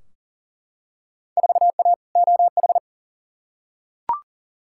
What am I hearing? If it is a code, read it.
Morse code, heard as a keyed tone, about 38 wpm, 700 Hz. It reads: 4A OH
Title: Field Day Exchanges - 40wpm